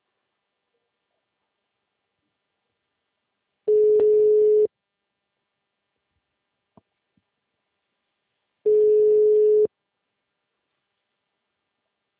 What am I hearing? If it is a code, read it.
Ringback tone (Europe, 425 Hz).